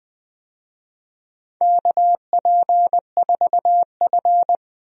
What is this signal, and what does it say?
Morse code, heard as a keyed tone, about 20 wpm, 700 Hz. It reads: KP4F
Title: DXCC-100 Call Signs - 4 Characters - 20wpm